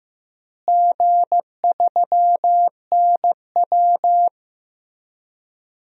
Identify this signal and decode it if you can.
Morse code, heard as a keyed tone, about 15 wpm, 700 Hz. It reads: G3NW
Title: DXCC-100 Call Signs - 4 Characters - Repeated 3x - 15wpm